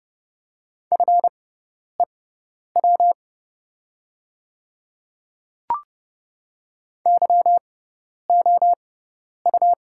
Morse code, heard as a keyed tone, about 30 wpm, spaced as for 10 wpm, 700 Hz. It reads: FEW YOU
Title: 6x Character Spacing - Top 300 Words - 30wpm